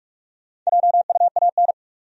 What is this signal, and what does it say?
Morse code, heard as a keyed tone, about 44 wpm, 700 Hz. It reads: JUAN
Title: USA Common Names - Max 4 Letters - 45wpm